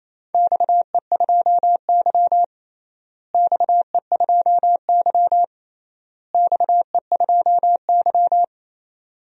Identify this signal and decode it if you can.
Morse code, heard as a keyed tone, about 28 wpm, 700 Hz. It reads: XE2Y XE2Y XE2Y
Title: DXCC-100 Call Signs - 4 Characters - Repeated 3x - 28wpm